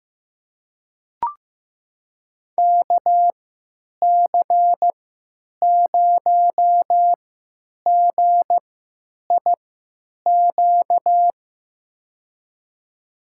Morse code, heard as a keyed tone, about 15 wpm, spaced as for 9 wpm, 700 Hz. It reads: KC0GIQ
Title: USA Call Signs - 6 Characters - 3x Character Spacing - 15wpm